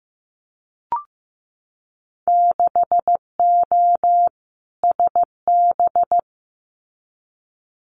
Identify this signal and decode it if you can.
Morse code, heard as a keyed tone, about 15 wpm, 700 Hz. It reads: 6O SB
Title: Winter Field Day Exchanges Encore 15wpm